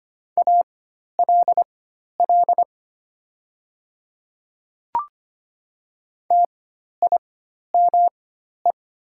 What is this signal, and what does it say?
Morse code, heard as a keyed tone, about 25 wpm, spaced as for 12 wpm, 700 Hz. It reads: ALL TIME
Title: 4x Character Spacing - Top 100 Words - 25wpm